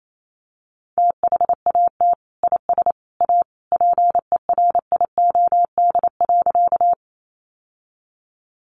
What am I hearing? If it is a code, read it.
Morse code, heard as a keyed tone, about 28 wpm, 700 Hz. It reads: THAT IS A PERIOD.
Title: Sentences from Top 800 Words 28wpm